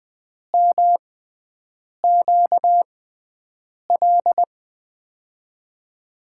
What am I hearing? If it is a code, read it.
Morse code, heard as a keyed tone, about 20 wpm, spaced as for 7 wpm, 700 Hz. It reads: MQL
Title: USA Call Signs - 6x Character Spacing - 20wpm